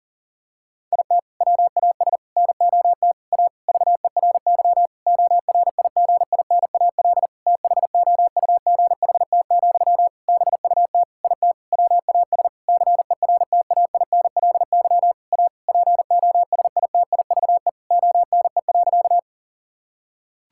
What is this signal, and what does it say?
Morse code, heard as a keyed tone, about 40 wpm, 700 Hz. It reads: IT WAS NOT A VERY ORIGINAL THOUGHT, BUT IT WAS CERTAINLY A POSITIVE ONE.
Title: Sentences from Top 1000 Words - Review Part 6 - 40wpm